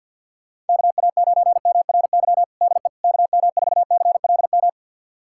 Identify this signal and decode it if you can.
Morse code, heard as a keyed tone, about 52 wpm, 700 Hz. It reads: KA9MRY DE KM4CLM
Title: USA Callsign DE Callsign - 6 Characters - 50wpm